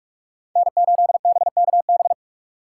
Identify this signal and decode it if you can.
Morse code, heard as a keyed tone, about 46 wpm, 700 Hz. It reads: N8BKB
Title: USA Call Signs - 5 Characters - 4x Character Spacing - 45wpm